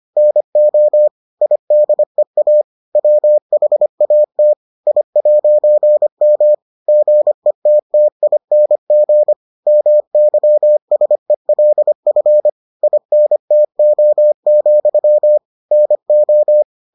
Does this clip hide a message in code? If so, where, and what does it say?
Yes: Morse code from 0.2 to 16.6 s, heard as a keyed tone, about 25 wpm, 600 Hz. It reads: NO IDEA WHAT I'M GETTING MYSELF INTO, NO